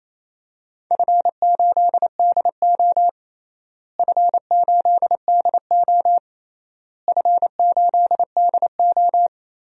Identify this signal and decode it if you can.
Morse code, heard as a keyed tone, about 28 wpm, 700 Hz. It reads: F8DO F8DO F8DO
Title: DXCC-100 Call Signs - 4 Characters - Repeated 3x - 28wpm